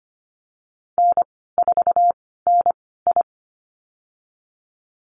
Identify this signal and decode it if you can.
Morse code, heard as a keyed tone, about 25 wpm, spaced as for 16 wpm, 700 Hz. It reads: N4NI
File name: Lesson 15g - Number 5 - Call Signs 25wpm